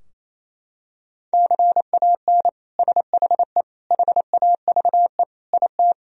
Morse code, heard as a keyed tone, about 28 wpm, 700 Hz. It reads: CAN SHE HAVE IT
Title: Sentences from Top 1000 Words - Review Part 4 - 28wpm